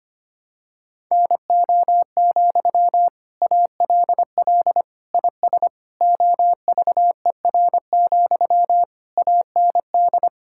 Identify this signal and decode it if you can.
Morse code, heard as a keyed tone, about 25 wpm, 700 Hz. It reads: NO, ALL IS OVER, AND
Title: Sentences from Top 1600 Words 25wpm